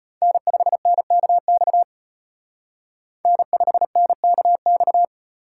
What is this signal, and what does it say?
Morse code, heard as a keyed tone, about 33 wpm, 700 Hz. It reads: N5NKX N5NKX